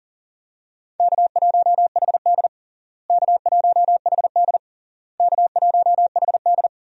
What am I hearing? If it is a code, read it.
Morse code, heard as a keyed tone, about 40 wpm, 700 Hz. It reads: K1HD K1HD K1HD
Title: USA Call Signs - 4 Characters - Repeated 3x - 40wpm